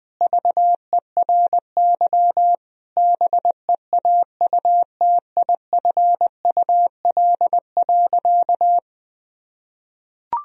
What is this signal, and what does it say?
Morse code, heard as a keyed tone, about 21 wpm, 700 Hz. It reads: VERY BEAUTIFUL.
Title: Sentences from Top 1000 Words - Review Part 1 - 20wpm